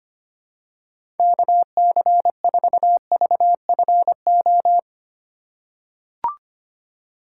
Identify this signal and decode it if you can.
Morse code, heard as a keyed tone, about 25 wpm, 700 Hz. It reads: KC4VFO
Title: USA Call Signs - 2x Character Spacing - 25wpm